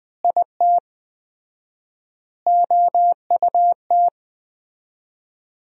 Morse code, heard as a keyed tone, about 20 wpm, 700 Hz. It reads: IT OUT
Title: 4x Word-Spacing - Sentences from Top 200 Words - 20wpm